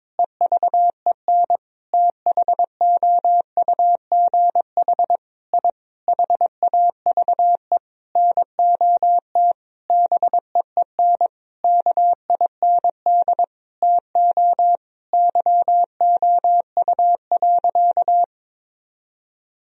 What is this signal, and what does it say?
Morse code, heard as a keyed tone, about 22 wpm, 700 Hz. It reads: EVEN THOUGH I HAVE NOT BEEN KIND TO YOU.
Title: Sentences from Top 1000 Words - Review Part 2 - 22wpm